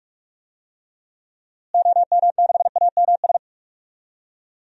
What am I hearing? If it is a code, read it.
Morse code, heard as a keyed tone, about 43 wpm, 700 Hz. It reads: OM6AMS
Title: DXCC-100 Call Signs - 6 Characters - Repeated 2x - 45wpm